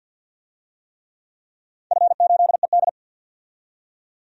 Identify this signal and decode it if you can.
Morse code, heard as a keyed tone, about 51 wpm, 700 Hz. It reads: F8ED
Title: DXCC-100 Call Signs - 4 Characters - 50wpm